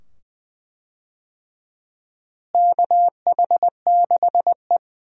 Morse code, heard as a keyed tone, about 21 wpm, 700 Hz. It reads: KH6E
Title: DXCC-100 Call Signs - 4 Characters - Repeated 2x - 20wpm